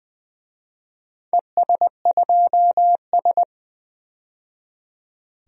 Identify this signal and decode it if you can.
Morse code, heard as a keyed tone, about 20 wpm, 700 Hz. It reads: ES2S
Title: DXCC-100 Call Signs - 4 Characters - 20wpm